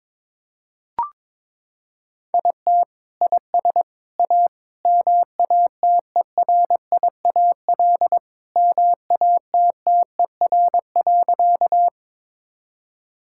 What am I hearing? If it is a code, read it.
Morse code, heard as a keyed tone, about 21 wpm, 700 Hz. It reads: IT IS A MATERIAL MATTER.
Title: Sentences from Top 1000 Words - Review Part 5 - 22wpm